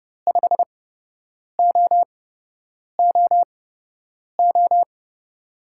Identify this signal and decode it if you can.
Morse code, heard as a keyed tone, about 30 wpm, spaced as for 8 wpm, 700 Hz. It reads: 5OOO
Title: USA Call Signs - 6 Characters - 8x Character Spacing - 30wpm